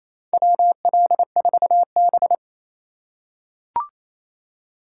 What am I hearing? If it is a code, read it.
Morse code, heard as a keyed tone, about 28 wpm, 700 Hz. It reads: WL4B
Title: USA Call Signs - 4 Characters - 8x Character Spacing - 28wpm